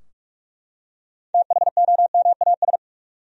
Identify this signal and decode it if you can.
Morse code, heard as a keyed tone, about 44 wpm, 700 Hz. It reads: THOMAS
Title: USA Common Names - 45wpm